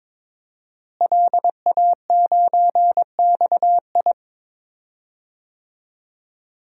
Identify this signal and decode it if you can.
Morse code, heard as a keyed tone, about 22 wpm, 700 Hz. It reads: LA9XI
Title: DXCC-100 Call Signs - 5 Characters - Repeated 3x - 22wpm